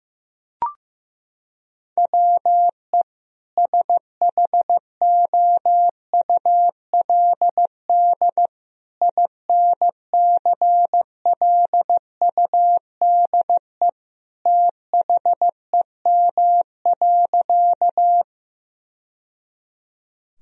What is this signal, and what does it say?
Morse code, heard as a keyed tone, about 16 wpm, 700 Hz. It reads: WE SHOULD INCLUDE THEM.
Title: Sentences from Top 1000 Words - Review Part 3 - 15wpm